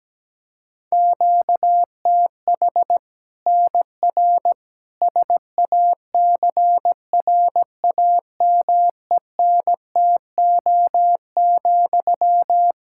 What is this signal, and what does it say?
Morse code, heard as a keyed tone, about 17 wpm, 700 Hz. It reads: QTH NR SACRAMENTO,
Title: Practice QSO - Exchange 1 - Beginning - 17wpm